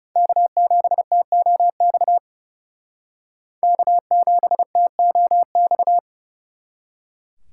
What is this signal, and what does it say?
Morse code, heard as a keyed tone, about 36 wpm, 700 Hz. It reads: K7TOX K7TOX
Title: Speed Racing - USA 1x3 Call Signs 30wpm